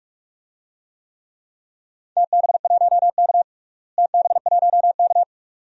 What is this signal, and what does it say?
Morse code, heard as a keyed tone, about 46 wpm, 700 Hz. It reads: TB1K TB1K
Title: DXCC-100 Call Signs - 4 Characters - Repeated 2x - 45wpm